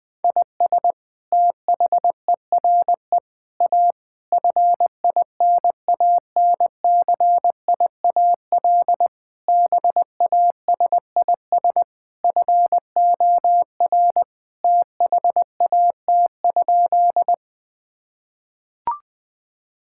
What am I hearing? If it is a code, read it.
Morse code, heard as a keyed tone, about 21 wpm, 700 Hz. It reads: IS THERE A FINANCIAL BASIS FOR THAT?
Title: Sentences from Top 1000 Words - Review Part 5 - 20wpm